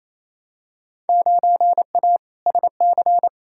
Morse code, heard as a keyed tone, about 28 wpm, 700 Hz. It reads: 9A SC
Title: Field Day Exchanges Encore - 28wpm